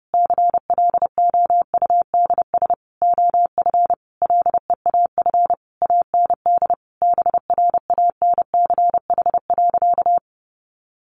Morse code, heard as a keyed tone, about 30 wpm, 700 Hz. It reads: CLOUDS OF LEAF AND BRANCH.
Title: A Sound Of Thunder 30wpm